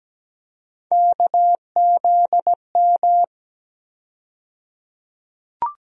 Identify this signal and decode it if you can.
Morse code, heard as a keyed tone, about 17 wpm, 700 Hz. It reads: KZM
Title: USA Call Sign Postfixes - 3 Letters 17wpm